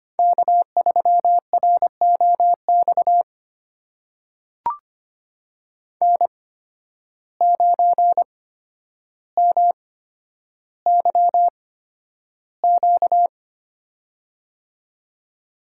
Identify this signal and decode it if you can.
Morse code, heard as a keyed tone, about 25 wpm, spaced as for 11 wpm, 700 Hz. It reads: K3ROX N9MYQ
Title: USA Call Signs - 5 Characters - 8x Character Spacing - 25wpm